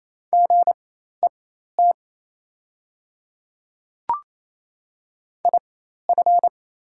Morse code, heard as a keyed tone, about 28 wpm, spaced as for 13 wpm, 700 Hz. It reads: GET IF
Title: 4x Character Spacing - Top 500 Words - 28wpm